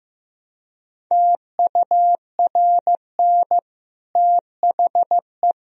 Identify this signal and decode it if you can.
Morse code, heard as a keyed tone, about 15 wpm, 700 Hz. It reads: TURN THE
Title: Sentences from Top 1000 Words - Review Part 1 - 15wpm